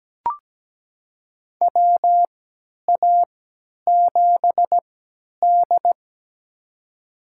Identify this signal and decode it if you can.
Morse code, heard as a keyed tone, about 17 wpm, spaced as for 10 wpm, 700 Hz. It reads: WA7D
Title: USA Call Signs - 4 Characters - 3x Character Spacing - 17wpm